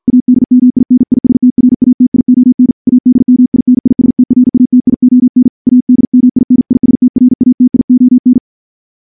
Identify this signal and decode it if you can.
Morse code, heard as a keyed tone, about 50 wpm, 250 Hz. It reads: ADMINISTRATION ADMINISTRATION ADMINISTRATION
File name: morse / morse-training-50WPM